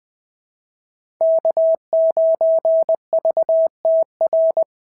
Morse code, heard as a keyed tone, about 20 wpm, 690 Hz. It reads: K9VTR
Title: SOTA QSO Practice 20wpm